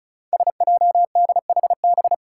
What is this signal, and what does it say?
Morse code, heard as a keyed tone, about 36 wpm, 700 Hz. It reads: SJDHB
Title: Five Challenging Letter-Number Combinations 35wpm